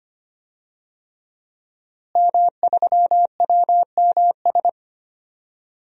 Morse code, heard as a keyed tone, about 25 wpm, 700 Hz. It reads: M3WMS
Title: DXCC-100 Call Signs - 5 Characters - Repeated 3x - 25wpm